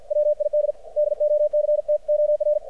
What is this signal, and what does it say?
Morse code, heard as a keyed tone, about 54 wpm, 600 Hz. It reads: WID DOCTOR
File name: archived / CRSnet